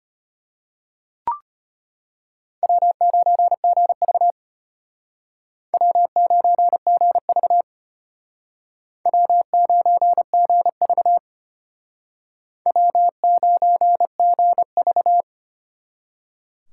Morse code, heard as a keyed tone, about 31 wpm, 700 Hz. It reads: W9GV W9GV W9GV W9GV